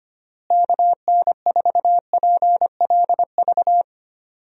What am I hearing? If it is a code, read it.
Morse code, heard as a keyed tone, about 25 wpm, 700 Hz. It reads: KN4PLV
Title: USA Call Signs - 6 Characters - Repeated 3x - 25wpm